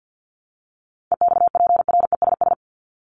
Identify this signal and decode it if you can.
Morse code, heard as a keyed tone, about 50 wpm, 700 Hz. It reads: EXPRESS
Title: ED Suffix - Top 100 Words with Base Words 50wpm